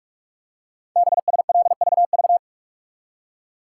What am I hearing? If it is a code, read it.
Morse code, heard as a keyed tone, about 44 wpm, 700 Hz. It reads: BSLVV
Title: Code Groups - 5 Letters - 45wpm